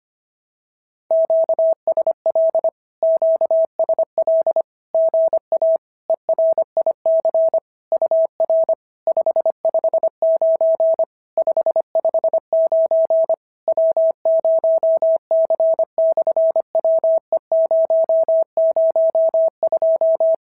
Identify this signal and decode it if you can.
Morse code, heard as a keyed tone, about 25 wpm, 650 Hz. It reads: QSL QSL GA ERIC UR 559 559 W0C/WE002
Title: SOTA QSO Practice 25wpm